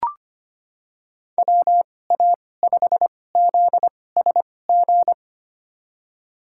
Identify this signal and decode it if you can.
Morse code, heard as a keyed tone, about 25 wpm, spaced as for 18 wpm, 700 Hz. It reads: WA5ZSG
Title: USA Call Signs - 6 Characters - 2x Character Spacing - 25wpm